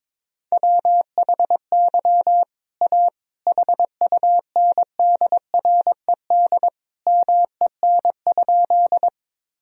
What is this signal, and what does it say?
Morse code, heard as a keyed tone, about 22 wpm, 700 Hz. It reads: WHY A HUNDRED MEN?
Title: Sentences from Top 1000 Words - Review Part 2 - 22wpm